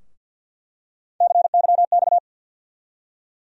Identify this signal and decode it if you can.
Morse code, heard as a keyed tone, about 48 wpm, 700 Hz. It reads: CYX
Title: USA Call Sign Postfixes - 3 Letters 50wpm